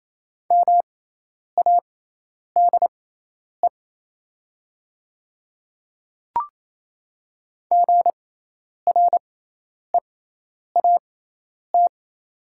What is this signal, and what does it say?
Morse code, heard as a keyed tone, about 28 wpm, spaced as for 10 wpm, 700 Hz. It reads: MADE GREAT
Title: 6x Character Spacing - Top 200 Words - 28wpm